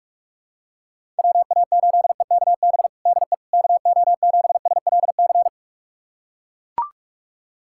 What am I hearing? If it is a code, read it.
Morse code, heard as a keyed tone, about 41 wpm, 700 Hz. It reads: WA8EKB DE KO7SLC
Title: USA Callsign DE Callsign - 6 Characters - 45wpm